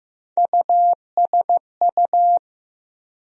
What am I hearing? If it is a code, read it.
Morse code, heard as a keyed tone, about 15 wpm, 700 Hz. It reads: USU
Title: USA Call Sign Postfixes - 3 Letters 15wpm